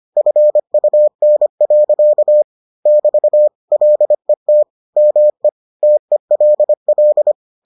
Morse code, heard as a keyed tone, about 25 wpm, 600 Hz. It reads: FUN. <BT> LET ME TELL